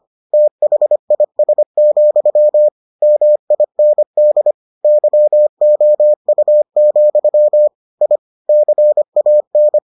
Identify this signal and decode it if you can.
Morse code, heard as a keyed tone, about 25 wpm, 600 Hz. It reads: THIS, MIND YOU, I CAN